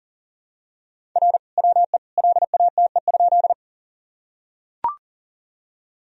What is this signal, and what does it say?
Morse code, heard as a keyed tone, about 37 wpm, 700 Hz. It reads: R WE LATE?
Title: Sentences from Top 300 Words and Abbreviations 40wpm